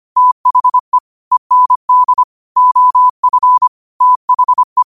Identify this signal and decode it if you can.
Morse code, heard as a keyed tone, about 25 wpm, 1000 Hz. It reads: THE END OF THE
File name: qotdfiles / 25wpm